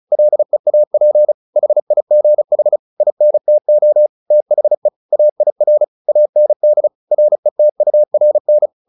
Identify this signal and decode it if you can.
Morse code, heard as a keyed tone, about 34 wpm, 600 Hz. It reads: LEAP HIGH INTO THE AIR AND RETURN
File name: the old man and the sea 2 35wpm